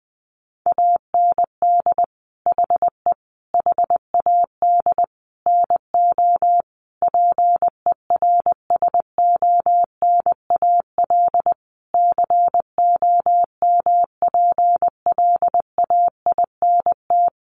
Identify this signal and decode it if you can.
Morse code, heard as a keyed tone, about 19 wpm, 700 Hz. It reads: AND HE HAD NO PERSONAL COMPLAINT
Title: Sentences from Top 1400 Words 20wpm